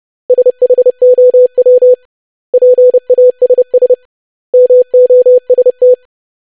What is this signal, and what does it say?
Morse code, heard as a keyed tone, about 30 wpm, 500 Hz. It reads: SHOW PASS MOST
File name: lesson / web / basic500-25wpm